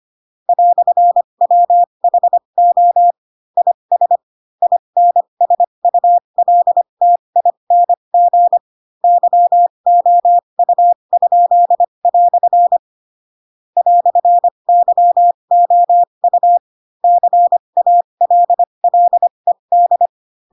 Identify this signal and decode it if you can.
Morse code, heard as a keyed tone, about 25 wpm, spaced as for 22 wpm, 700 Hz. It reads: "WHO IS INSULTING YOU?" "YOU CALLED
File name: Pinocchio / Pinocchio ch2 18 25 ik1hge